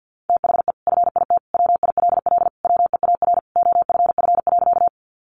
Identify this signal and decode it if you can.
Morse code, heard as a keyed tone, about 50 wpm, 700 Hz. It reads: THE FIT WILL WEAR OFF.
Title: Sentences from Top 1500 Words 50wpm